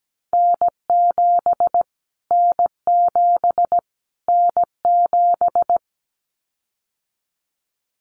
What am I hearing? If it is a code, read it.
Morse code, heard as a keyed tone, about 17 wpm, 700 Hz. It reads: N7 N7 N7
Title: Callsign Prefixes for Top 25 DXCC - Repeated 3x - 17wpm